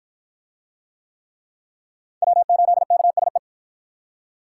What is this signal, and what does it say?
Morse code, heard as a keyed tone, about 47 wpm, 700 Hz. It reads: W8KSE